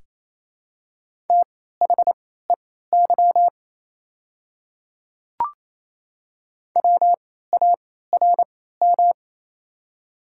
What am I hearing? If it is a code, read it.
Morse code, heard as a keyed tone, about 28 wpm, spaced as for 16 wpm, 700 Hz. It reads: THEY WARM
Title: 3x Character Spacing - Top 500 Words - 28wpm